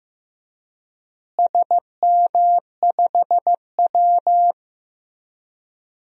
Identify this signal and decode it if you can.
Morse code, heard as a keyed tone, about 15 wpm, 700 Hz. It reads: SM5W
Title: DXCC-100 Call Signs - 4 Characters - Repeated 3x - 15wpm